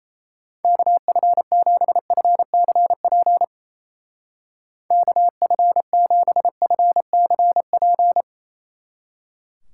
Morse code, heard as a keyed tone, about 35 wpm, 700 Hz. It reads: KF7FCP KF7FCP
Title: Speed Racing - USA Call Signs 28wpm